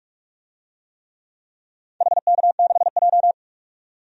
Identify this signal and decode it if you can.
Morse code, heard as a keyed tone, about 46 wpm, 700 Hz. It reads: HK6J
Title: DXCC-100 Call Signs - 4 Characters - 45wpm